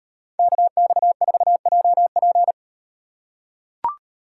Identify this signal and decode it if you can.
Morse code, heard as a keyed tone, about 35 wpm, 700 Hz. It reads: KX4JP